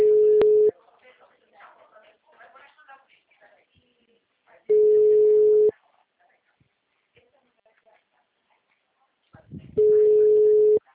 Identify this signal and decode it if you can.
Ringback tone (Europe, 425 Hz), with speech.